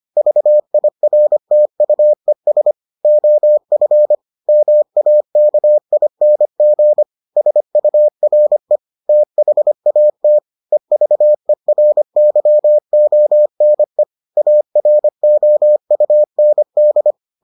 Morse code, heard as a keyed tone, about 25 wpm, 600 Hz. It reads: VIRTUES OF MAKING SURE THAT EVERYONE AROUND